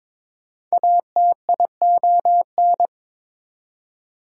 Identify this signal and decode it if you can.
Morse code, heard as a keyed tone, about 22 wpm, 700 Hz. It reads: ATION
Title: Prefixes and Suffixes 22wpm